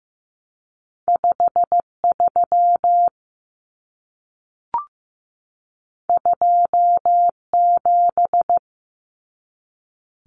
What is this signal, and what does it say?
Morse code, heard as a keyed tone, about 15 wpm, 700 Hz. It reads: 53 27
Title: Common Cut Numbers - 2-Digit 15wpm